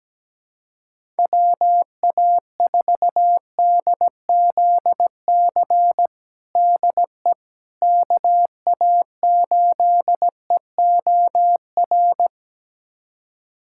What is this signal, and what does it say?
Morse code, heard as a keyed tone, about 17 wpm, 700 Hz. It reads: WA4DZC DE KA8EOR
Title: USA Callsign DE Callsign - 6 Characters - 17wpm